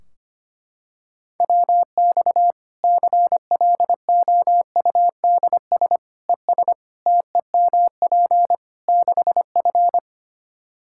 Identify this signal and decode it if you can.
Morse code, heard as a keyed tone, about 25 wpm, 700 Hz. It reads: WX CLOUDS ES TEMP 6F
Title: Practice QSO - Exchange 2 - Info - 25wpm